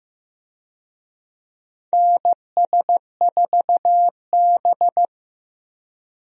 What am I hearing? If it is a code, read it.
Morse code, heard as a keyed tone, about 15 wpm, 700 Hz. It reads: NS4B
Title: USA Call Signs - 4 Characters - Repeated 3x - 15wpm